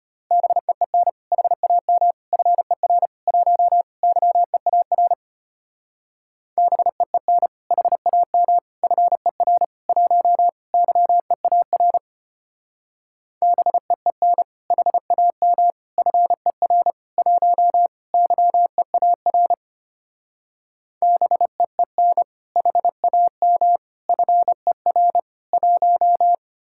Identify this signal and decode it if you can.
Morse code, heard as a keyed tone, about 32 wpm, 700 Hz. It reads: BEEN HAM FER 1 YEAR BEEN HAM FER 1 YEAR BEEN HAM FER 1 YEAR BEEN HAM FER 1